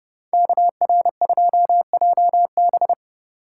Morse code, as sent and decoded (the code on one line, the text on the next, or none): -.- .-. ..--- .--- -...
KR2JB